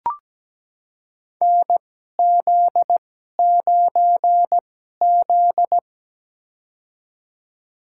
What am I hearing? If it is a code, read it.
Morse code, heard as a keyed tone, about 17 wpm, spaced as for 12 wpm, 700 Hz. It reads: NZ9Z
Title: USA Call Signs - 2x Character Spacing - 17wpm